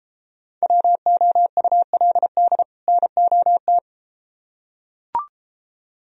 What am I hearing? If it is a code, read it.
Morse code, heard as a keyed tone, about 33 wpm, 700 Hz. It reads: WOULD NOT